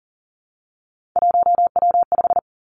Morse code, heard as a keyed tone, about 40 wpm, 700 Hz. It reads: 1W5
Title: Three Challenging Letter-Number Combinations 40wpm